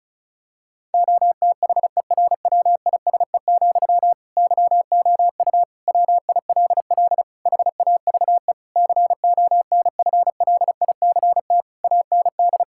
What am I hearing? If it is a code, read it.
Morse code, heard as a keyed tone, about 36 wpm, 700 Hz. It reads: OTHERWISE, YOU WILL HAVE CONFLICT AND
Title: Sentences from Top 1600 Words 35wpm